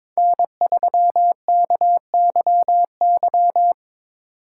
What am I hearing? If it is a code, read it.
Morse code, heard as a keyed tone, about 22 wpm, 700 Hz. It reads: N3KYY
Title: USA Call Signs - 5 Characters - 2x Character Spacing - 22wpm